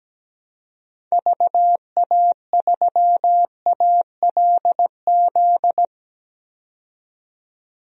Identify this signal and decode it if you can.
Morse code, heard as a keyed tone, about 17 wpm, 700 Hz. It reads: VA3ALZ
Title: DXCC-100 Call Signs - 6 Characters - 17wpm